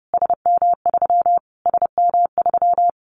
Morse code, heard as a keyed tone, about 30 wpm, 700 Hz. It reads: SM3 SM3
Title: Callsign Prefixes for Top 25 DXCC - Repeated 3x - 30wpm